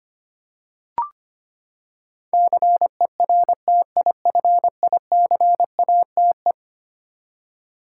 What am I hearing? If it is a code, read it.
Morse code, heard as a keyed tone, about 25 wpm, 700 Hz. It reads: CERTIFICATE
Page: morse-code-ninja-practice-25wpm